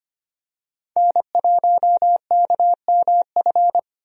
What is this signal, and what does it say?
Morse code, heard as a keyed tone, about 25 wpm, 700 Hz. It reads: N1KMF
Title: USA Call Signs - 5 Characters - 25wpm